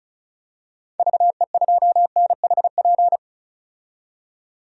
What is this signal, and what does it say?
Morse code, heard as a keyed tone, about 36 wpm, 700 Hz. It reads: VE2NHP
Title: DXCC-100 Call Signs - 6 Characters - 35wpm